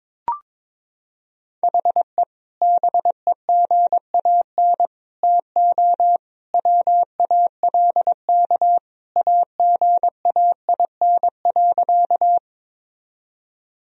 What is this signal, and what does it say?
Morse code, heard as a keyed tone, about 22 wpm, 700 Hz. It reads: HE BEGAN TO WALK AGAIN.
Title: Sentences from Top 1000 Words - Review Part 1 - 22wpm